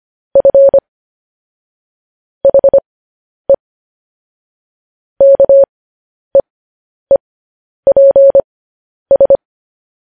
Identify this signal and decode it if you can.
Morse code, heard as a keyed tone, about 25 wpm, spaced as for 10 wpm, 550 Hz. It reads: F HE KEEPS